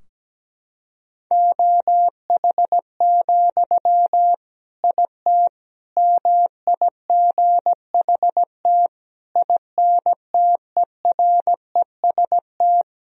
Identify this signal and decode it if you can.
Morse code, heard as a keyed tone, about 17 wpm, 700 Hz. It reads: OH, IT MIGHT INTEREST
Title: Sentences from Top 1000 Words - Review Part 2 - 17wpm